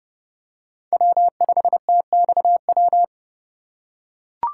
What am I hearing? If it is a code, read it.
Morse code, heard as a keyed tone, about 29 wpm, 700 Hz. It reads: W5TXW
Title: USA Call Signs - 3x Character Spacing - 30wpm